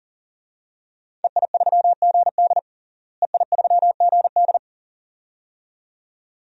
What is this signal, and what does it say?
Morse code, heard as a keyed tone, about 40 wpm, 700 Hz. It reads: EI3GD EI3GD
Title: DXCC-100 Call Signs - 5 Characters - Repeated 2x - 40wpm